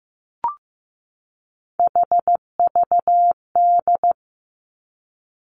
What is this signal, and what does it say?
Morse code, heard as a keyed tone, about 15 wpm, 700 Hz. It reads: HVD
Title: Three Challenging Letter-Number Combinations 15wpm